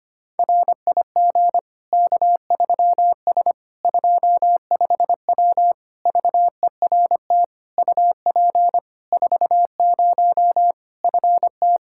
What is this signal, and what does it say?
Morse code, heard as a keyed tone, about 25 wpm, 700 Hz. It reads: RIG K3S 25W VERT UP 40 FT
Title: K1USN SST QSO Practice 25wpm